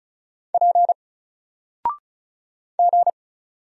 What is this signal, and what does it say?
Morse code, heard as a keyed tone, about 36 wpm, 700 Hz. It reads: P G
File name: Single-Letter-Number-35wpm